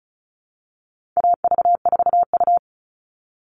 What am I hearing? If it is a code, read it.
Morse code, heard as a keyed tone, about 34 wpm, 700 Hz. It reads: AV4U
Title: All Cut Numbers - 4-Digit 35wpm